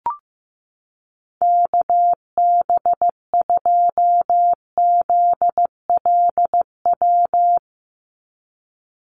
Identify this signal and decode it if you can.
Morse code, heard as a keyed tone, about 15 wpm, 700 Hz. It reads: KB2ZLW
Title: USA Call Signs 15wpm